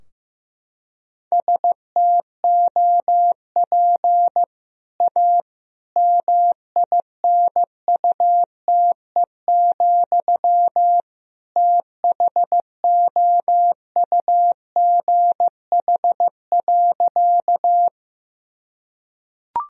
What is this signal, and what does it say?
Morse code, heard as a keyed tone, about 15 wpm, 700 Hz. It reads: STOP A MINUTE, THOUGH.
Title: Sentences from Top 1000 Words - Review Part 2 - 15wpm